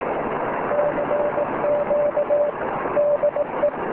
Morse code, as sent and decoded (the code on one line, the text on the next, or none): -.-. --.- / -.. .
CQ DE